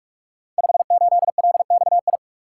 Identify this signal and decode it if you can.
Morse code, heard as a keyed tone, about 44 wpm, 700 Hz. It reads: 58LXI
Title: Code Groups - 5 Letter:Numbers - 45wpm